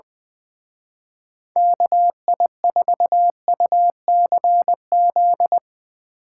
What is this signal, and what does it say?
Morse code, heard as a keyed tone, about 25 wpm, 700 Hz. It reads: KI4UCZ
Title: Speed Racing - USA 2x3 Call Signs 20wpm